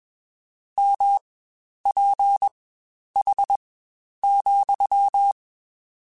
Morse code, heard as a keyed tone, about 21 wpm, spaced as for 10 wpm, 800 Hz. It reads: MPH,
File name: Spaced Characters / Spaced Characters-mp3s / Weather5 1 sp @20